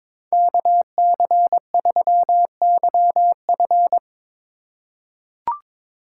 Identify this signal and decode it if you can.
Morse code, heard as a keyed tone, about 23 wpm, 700 Hz. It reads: KC3YF
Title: USA Call Signs - 3x Character Spacing - 22wpm